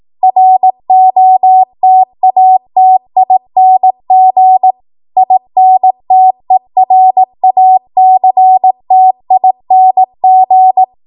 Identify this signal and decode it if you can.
Morse code, heard as a keyed tone, about 18 wpm, 750 Hz. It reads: ROTATING INTERACTING
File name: Morse / 18WPM